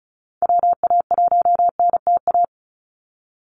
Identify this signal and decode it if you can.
Morse code, heard as a keyed tone, about 34 wpm, 700 Hz. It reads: WA1NTA
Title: Call Signs Repeated 2x - 35wpm